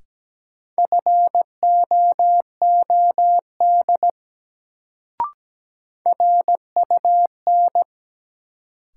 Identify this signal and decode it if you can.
Morse code, heard as a keyed tone, about 17 wpm, 700 Hz. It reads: FOOD RUN
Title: Sentences from Top 300 Words - New Word Drill - 17wpm